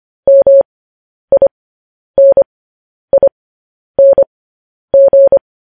Morse code, heard as a keyed tone, about 25 wpm, spaced as for 10 wpm, 550 Hz. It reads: MINING
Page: news headlines in morse code at 10 wpm